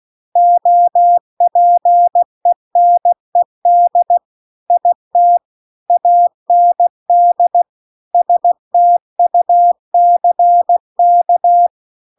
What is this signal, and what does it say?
Morse code, heard as a keyed tone, about 16 wpm, 700 Hz. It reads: OPENED IT AND STUCK